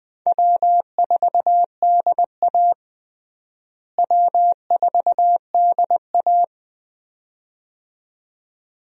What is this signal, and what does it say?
Morse code, heard as a keyed tone, about 20 wpm, 700 Hz. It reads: W4DA W4DA
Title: USA Call Signs - 4 Characters - Repeated 2x - 20wpm